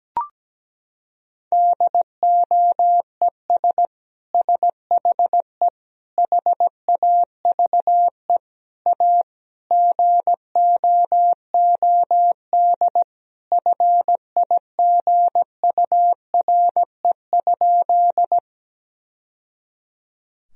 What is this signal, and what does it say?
Morse code, heard as a keyed tone, about 17 wpm, 700 Hz. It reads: DOES SHE HAVE A GOOD FIGURE?
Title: Sentences from Top 1000 Words - Review Part 2 - 17wpm